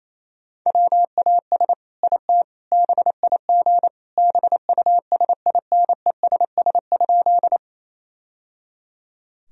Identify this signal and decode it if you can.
Morse code, heard as a keyed tone, about 28 wpm, 700 Hz. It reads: WAS IT BIG BUSINESS?
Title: Sentences from Top 1000 Words - Review Part 1 - 28wpm